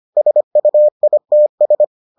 Morse code, heard as a keyed tone, about 25 wpm, 600 Hz. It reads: SUITS